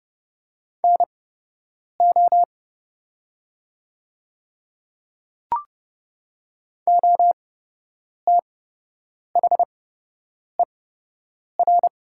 Morse code, heard as a keyed tone, about 30 wpm, spaced as for 8 wpm, 700 Hz. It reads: NO OTHER
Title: 8x Character Spacing - Top 200 Words - 30wpm